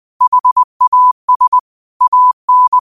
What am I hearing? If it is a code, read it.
Morse code, heard as a keyed tone, about 20 wpm, 1000 Hz. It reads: HAS AN